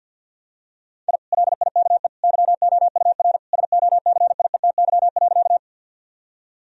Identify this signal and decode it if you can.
Morse code, heard as a keyed tone, about 50 wpm, 700 Hz. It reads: I LIKE YOUR SOCIETY.
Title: Sentences from Top 1000 Words - Review Part 4 - 50wpm